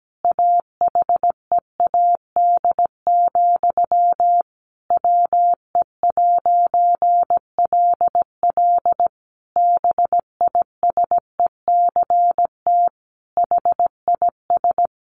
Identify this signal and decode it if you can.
Morse code, heard as a keyed tone, about 17 wpm, 700 Hz. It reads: AHEAD, WE'LL BISECT HIS
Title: A Sound Of Thunder 17wpm